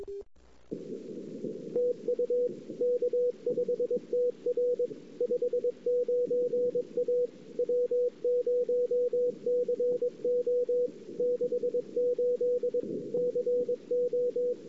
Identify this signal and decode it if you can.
Morse code, heard as a keyed tone, about 22 wpm, 490 Hz. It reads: TU K5TR 59 A W0CO 68 CO
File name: CW / 2008-ss-cw-k5tr-op-n5rz -06:00